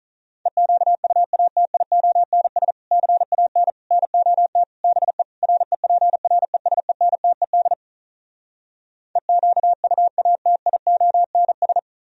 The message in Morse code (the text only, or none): EQUATIONS CAN NOT BE REPRESENTED EQUATIONS